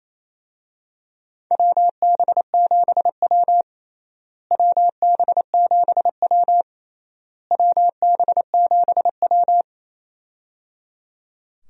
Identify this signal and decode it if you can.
Morse code, heard as a keyed tone, about 28 wpm, 700 Hz. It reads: WB7W WB7W WB7W
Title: USA Call Signs - 4 Characters - Repeated 3x - 28wpm